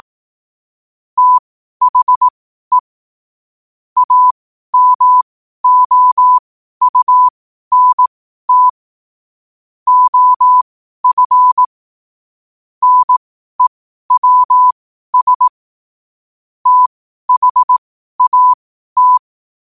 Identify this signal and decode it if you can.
Morse code, heard as a keyed tone, about 18 wpm, spaced as for 12 wpm, 1000 Hz. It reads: THE AMOUNT OF NEWS THAT
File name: qotdfiles / 12wpm